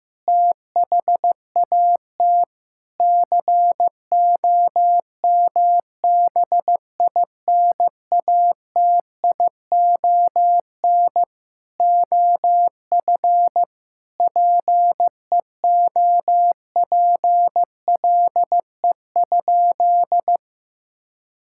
Morse code, heard as a keyed tone, about 16 wpm, 700 Hz. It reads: THAT COMBINATION OF PEOPLE?
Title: Sentences from Top 1000 Words - Review Part 6 - 15wpm